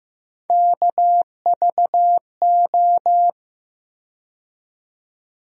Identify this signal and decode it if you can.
Morse code, heard as a keyed tone, about 15 wpm, 700 Hz. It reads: KVO
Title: USA Call Sign Postfixes - 3 Letters 15wpm